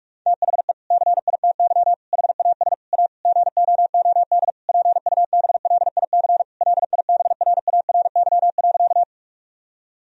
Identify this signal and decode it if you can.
Morse code, heard as a keyed tone, about 43 wpm, 700 Hz. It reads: THE CITY HAS A GOOD PUBLIC LIBRARY.
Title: Sentences from Top 1000 Words - Review Part 4 - 45wpm